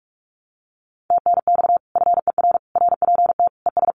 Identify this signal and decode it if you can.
Morse code, heard as a keyed tone, about 46 wpm, 700 Hz. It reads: TNX FER RPT ES
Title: Common QSO Phrases Encore 45wpm